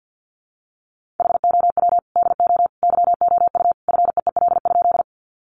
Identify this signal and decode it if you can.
Morse code, heard as a keyed tone, about 50 wpm, 700 Hz. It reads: HOW DO YOU FEEL?
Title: Sentences from Top 400 Words 50wpm